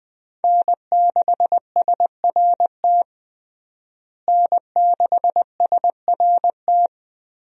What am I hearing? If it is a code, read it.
Morse code, heard as a keyed tone, about 20 wpm, 700 Hz. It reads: N6SRT N6SRT
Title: USA Call Signs - 5 Characters - Repeated 2x - 20wpm